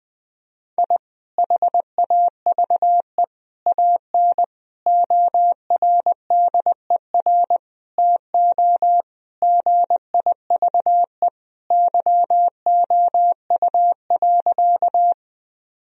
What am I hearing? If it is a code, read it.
Morse code, heard as a keyed tone, about 21 wpm, 700 Hz. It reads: I HAVE AN ORDER TO GIVE YOU.
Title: Sentences from Top 1000 Words - Review Part 2 - 20wpm